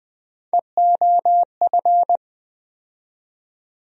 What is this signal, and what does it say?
Morse code, heard as a keyed tone, about 21 wpm, 700 Hz. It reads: EOF
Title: USA Call Sign Postfixes - 3 Letters 20wpm